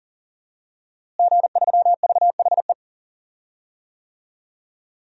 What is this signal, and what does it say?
Morse code, heard as a keyed tone, about 39 wpm, 700 Hz. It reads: G3VHE
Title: DXCC-100 Call Signs - 5 Characters - Repeated 2x - 40wpm